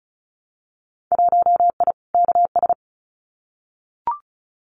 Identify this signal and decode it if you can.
Morse code, heard as a keyed tone, about 36 wpm, 700 Hz. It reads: 1I KS
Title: Winter Field Day Exchanges Encore 35wpm